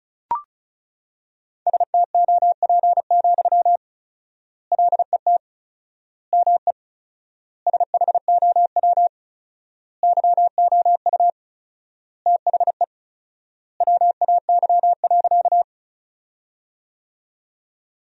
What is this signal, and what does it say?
Morse code, heard as a keyed tone, about 36 wpm, 700 Hz. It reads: STOP, LET ME SHOW YOU THE WAY.
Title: 4x Word-Spacing - Sentences from Top 300 Words - 35wpm